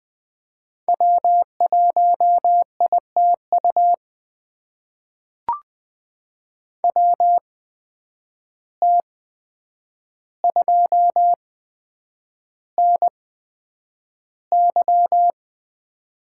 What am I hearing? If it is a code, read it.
Morse code, heard as a keyed tone, about 20 wpm, spaced as for 9 wpm, 700 Hz. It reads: W1ITU WT2NY
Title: USA Call Signs - 5 Characters - 8x Character Spacing - 20wpm